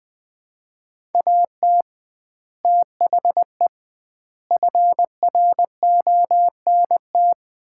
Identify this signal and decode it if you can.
Morse code, heard as a keyed tone, about 21 wpm, 700 Hz. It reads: AT THE FRONT
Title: 2x Word-Spacing - Sets of 5 Words - 20wpm